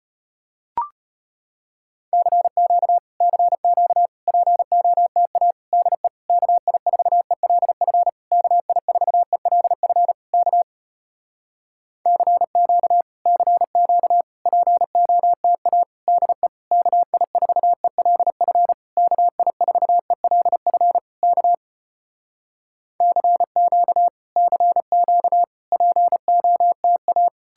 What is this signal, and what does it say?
Morse code, heard as a keyed tone, about 33 wpm, 700 Hz. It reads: CQ CQ POTA DE KI4ELF KI4ELF K CQ CQ POTA DE KI4ELF KI4ELF K CQ CQ POTA